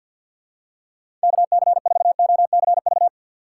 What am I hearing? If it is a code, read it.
Morse code, heard as a keyed tone, about 50 wpm, 700 Hz. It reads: KC4OCV
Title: USA Call Signs - 6 Characters - 3x Character Spacing - 50wpm